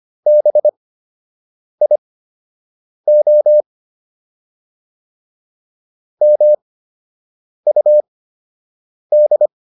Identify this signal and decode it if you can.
Morse code, heard as a keyed tone, about 25 wpm, spaced as for 7 wpm, 600 Hz. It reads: BIO MUD